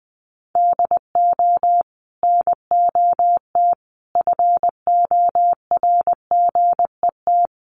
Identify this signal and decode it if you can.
Morse code, heard as a keyed tone, about 19 wpm, 700 Hz. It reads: DO NOT FORGET
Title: Sentences from Top 1200 Words 20wpm